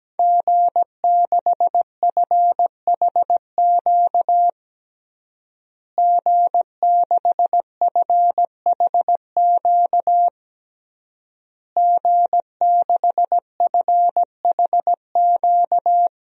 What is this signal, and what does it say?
Morse code, heard as a keyed tone, about 17 wpm, 700 Hz. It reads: G6FHQ G6FHQ G6FHQ
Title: DXCC-100 Call Signs - 5 Characters - Repeated 3x - 17wpm